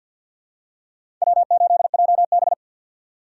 Morse code, heard as a keyed tone, about 45 wpm, 700 Hz. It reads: W8JB
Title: USA Call Signs - 2x Character Spacing - 50wpm